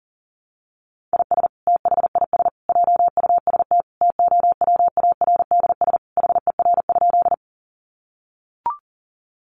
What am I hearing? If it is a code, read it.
Morse code, heard as a keyed tone, about 40 wpm, 700 Hz. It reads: IS THIS JUST TOWARDS HER?
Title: Sentences from Top 1100 Words 40wpm